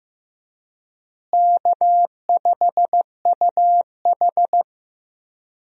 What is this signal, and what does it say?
Morse code, heard as a keyed tone, about 15 wpm, 700 Hz. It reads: K5UH
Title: USA Call Signs - 4 Characters - Repeated 2x - 15wpm